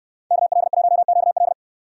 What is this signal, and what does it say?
Morse code, heard as a keyed tone, about 68 wpm, 700 Hz. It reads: KL1QL